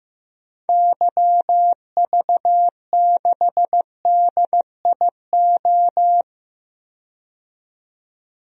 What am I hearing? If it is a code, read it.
Morse code, heard as a keyed tone, about 15 wpm, 700 Hz. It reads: YV6DIO
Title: DXCC-100 Call Signs - 6 Characters - 15wpm